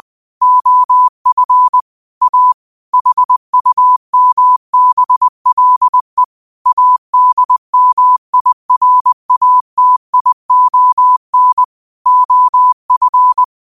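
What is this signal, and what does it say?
Morse code, heard as a keyed tone, about 20 wpm, 1000 Hz. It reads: OF A HUMBLE ADMIRATION OF
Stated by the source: Quote of the day sent in Morse at 20 words per minute.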